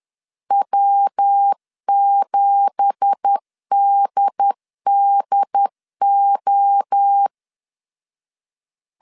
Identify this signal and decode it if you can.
Morse code, heard as a keyed tone, about 10 wpm, 800 Hz. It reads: W7DDO
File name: CW / DET / 1 9 1500C S B 10wpm 4439